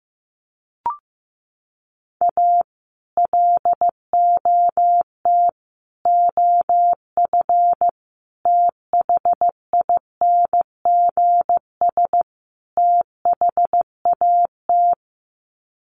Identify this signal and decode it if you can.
Morse code, heard as a keyed tone, about 15 wpm, 700 Hz. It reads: A LOT OF THINGS THAT
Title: Sets of 5 Words 15wpm